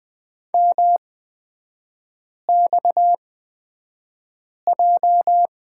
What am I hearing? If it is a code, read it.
Morse code, heard as a keyed tone, about 20 wpm, spaced as for 5 wpm, 700 Hz. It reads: MXJ
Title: 250 Lesson 40e - BK - Three Characters 20wpm